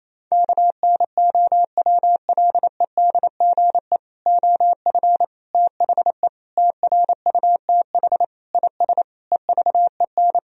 Morse code, heard as a keyed tone, about 28 wpm, 700 Hz. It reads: KNOWLEDGE OF THE TRUTH IS EVEN
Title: Sentences from Top 1000 Words - Review Part 3 - 28wpm